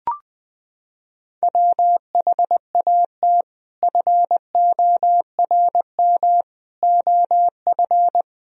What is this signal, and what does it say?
Morse code, heard as a keyed tone, about 20 wpm, 700 Hz. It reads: WHAT FORM OF
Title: Sentences from Top 1000 Words - Review Part 3 - 20wpm